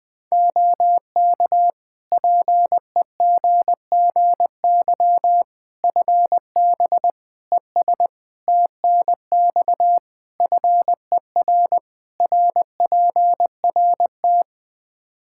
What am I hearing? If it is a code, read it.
Morse code, heard as a keyed tone, about 20 wpm, 700 Hz. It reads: OK PEGGY FB ES TNX FER RPRT
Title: Practice QSO - Exchange 2 - Info - 20wpm